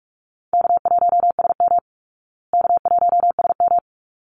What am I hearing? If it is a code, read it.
Morse code, heard as a keyed tone, about 44 wpm, 700 Hz. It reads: K1SM K1SM
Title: Call Signs Repeated 3x - 45wpm